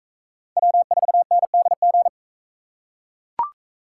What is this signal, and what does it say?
Morse code, heard as a keyed tone, about 38 wpm, 700 Hz. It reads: W4NDG